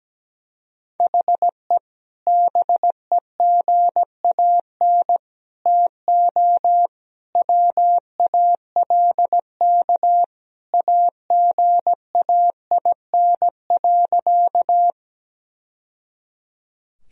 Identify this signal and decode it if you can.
Morse code, heard as a keyed tone, about 17 wpm, 700 Hz. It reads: HE BEGAN TO WALK AGAIN.
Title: Sentences from Top 1000 Words - Review Part 1 - 17wpm